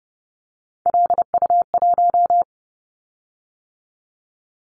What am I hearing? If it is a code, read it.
Morse code, heard as a keyed tone, about 31 wpm, 700 Hz. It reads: LU1
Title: Callsign Prefixes for Top 50 DXCC - 30wpm